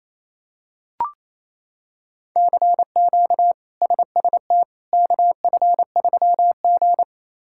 Morse code, heard as a keyed tone, about 27 wpm, 700 Hz. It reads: CQ SST KF3G
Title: K1USN SST QSO Practice - 28wpm